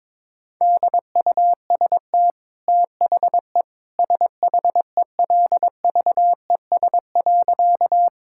Morse code, heard as a keyed tone, about 22 wpm, 700 Hz. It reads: DUST THE SHELVES.
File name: practice / 22wpm / Easy Sentences with Hints - 22wpm